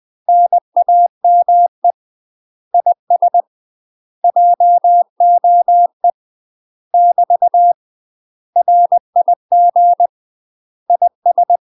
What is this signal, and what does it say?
Morse code, heard as a keyed tone, about 20 wpm, 700 Hz. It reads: NAME IS JOE <BT> RIG IS
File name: UK MASTER QSOs TRAINING/20WPM